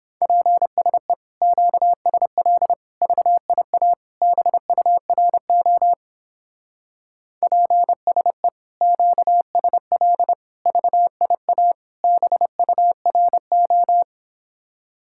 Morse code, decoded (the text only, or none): PSE QSL VIA BURO PSE QSL VIA BURO